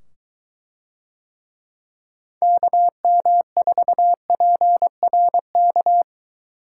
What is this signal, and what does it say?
Morse code, heard as a keyed tone, about 23 wpm, 700 Hz. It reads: KM4PRK